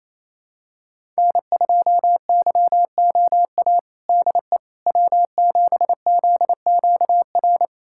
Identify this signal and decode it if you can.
Morse code, heard as a keyed tone, about 28 wpm, 700 Hz. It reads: N2YOA DE W7ZQR
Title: USA Callsign DE Callsign - 5 Characters - 28wpm